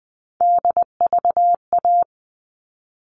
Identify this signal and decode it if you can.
Morse code, heard as a keyed tone, about 20 wpm, 700 Hz. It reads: DVA
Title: All Cut Numbers - 3-Digit 20wpm